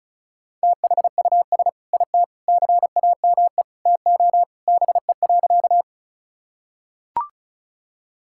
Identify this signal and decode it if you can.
Morse code, heard as a keyed tone, about 36 wpm, 700 Hz. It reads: THUS IT CAME TO BE.
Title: Sentences from Top 1000 Words - Review Part 5 - 35wpm